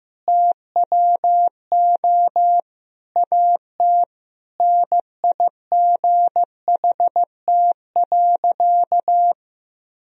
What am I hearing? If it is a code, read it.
Morse code, heard as a keyed tone, about 16 wpm, 700 Hz. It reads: TWO AT NIGHT.
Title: Sentences from Top 1000 Words - Review Part 1 - 15wpm